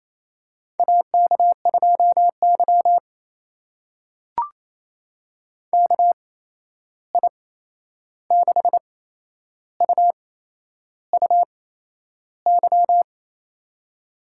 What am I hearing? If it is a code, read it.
Morse code, heard as a keyed tone, about 28 wpm, spaced as for 11 wpm, 700 Hz. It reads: AK2Y KI6UUY
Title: USA Call Signs - 8x Character Spacing - 28wpm